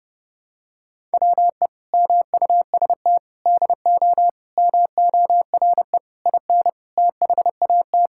Morse code, heard as a keyed tone, about 30 wpm, 700 Hz. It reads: WE MUST DO MORE IN THAT
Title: Sentences from Top 1000 Words - Review Part 5 - 30wpm